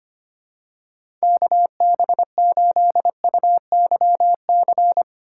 Morse code, heard as a keyed tone, about 25 wpm, 700 Hz. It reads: KB8UYC
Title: USA Call Signs - 4x Character Spacing - 25wpm